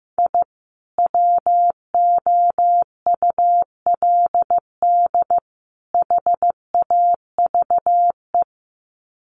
Morse code, heard as a keyed tone, about 15 wpm, 700 Hz. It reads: I WOULD HAVE
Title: Sets of 3 Words 15wpm